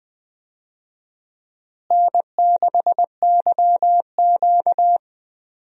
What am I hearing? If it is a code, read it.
Morse code, heard as a keyed tone, about 20 wpm, 700 Hz. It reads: N6YQ
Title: USA Call Signs - 4 Characters - Repeated 3x - 20wpm